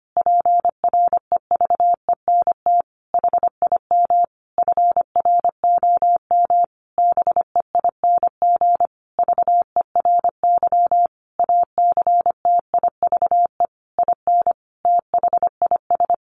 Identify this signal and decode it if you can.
Morse code, heard as a keyed tone, about 25 wpm, 700 Hz. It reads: PREVENT HIM FROM BEING VERY ACTIVE IN THIS
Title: Sentences from Top 1500 Words 25wpm